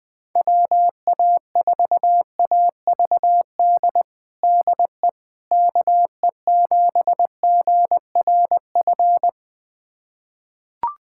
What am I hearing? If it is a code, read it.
Morse code, heard as a keyed tone, about 20 wpm, 700 Hz. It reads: WA4AVD DE KE7GRF
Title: USA Callsign DE Callsign 20wpm